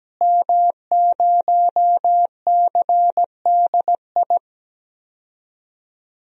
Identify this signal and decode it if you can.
Morse code, heard as a keyed tone, about 17 wpm, 700 Hz. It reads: M0CDI
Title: DXCC-100 Call Signs - 5 Characters - Repeated 3x - 17wpm